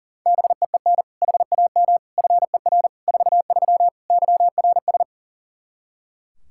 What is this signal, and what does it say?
Morse code, heard as a keyed tone, about 40 wpm, 700 Hz. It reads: BEEN HAM FER 43 YRS
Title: Practice QSO - Exchange 3 - More Info - 40wpm